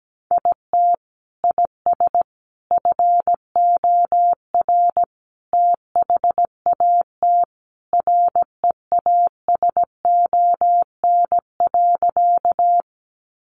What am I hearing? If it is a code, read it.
Morse code, heard as a keyed tone, about 17 wpm, 700 Hz. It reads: IT IS FOR THAT REASON.
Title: Sentences from Top 700 Words 17wpm